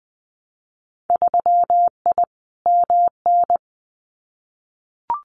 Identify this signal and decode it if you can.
Morse code, heard as a keyed tone, about 20 wpm, 700 Hz. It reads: 3I MN
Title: Winter Field Day Exchanges 20wpm